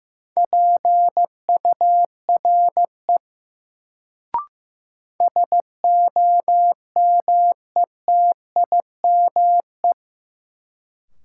Morse code, heard as a keyed tone, about 16 wpm, 700 Hz. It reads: PURE SOMETIME
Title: Sentences from Top 1500 Words - New Word Drill Alt - 15wpm